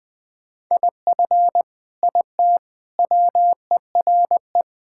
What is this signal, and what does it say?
Morse code, heard as a keyed tone, about 20 wpm, 700 Hz. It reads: IF IT WERE
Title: Sentences with 5-Word Phrases - 20wpm